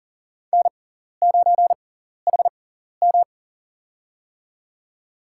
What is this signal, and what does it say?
Morse code, heard as a keyed tone, about 40 wpm, spaced as for 14 wpm, 700 Hz. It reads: N9HM
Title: USA Call Signs - 6x Character Spacing - 40wpm